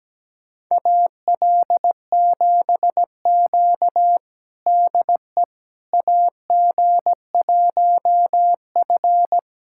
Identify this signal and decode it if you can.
Morse code, heard as a keyed tone, about 17 wpm, 700 Hz. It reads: AL7Q DE AG1F
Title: USA Callsign DE Callsign - 4 Characters - 17wpm